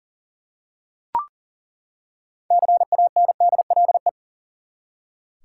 Morse code, heard as a keyed tone, about 40 wpm, 700 Hz. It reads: CANDLE
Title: Six Letter Words - 40wpm